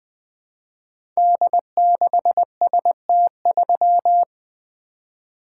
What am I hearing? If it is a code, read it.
Morse code, heard as a keyed tone, about 20 wpm, 700 Hz. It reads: D6ST3
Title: Code Groups - 5 Letter:Numbers - 20wpm